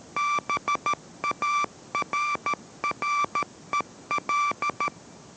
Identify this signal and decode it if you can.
Morse code, heard as a keyed tone, about 13 wpm, 1170 Hz. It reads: BARREL